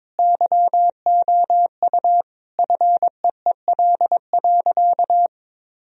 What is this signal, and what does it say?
Morse code, heard as a keyed tone, about 22 wpm, 700 Hz. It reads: YOU FEEL.
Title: Sentences from Top 1000 Words - Review Part 1 - 22wpm